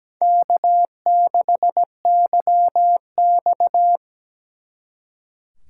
Morse code, heard as a keyed tone, about 20 wpm, 700 Hz. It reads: K6YX
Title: Speed Racing - USA Call Signs 17wpm